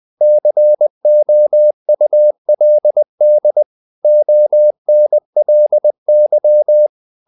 Morse code, heard as a keyed tone, about 20 wpm, 600 Hz. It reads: COULD ONLY
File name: the old man and the sea part 2 20wpm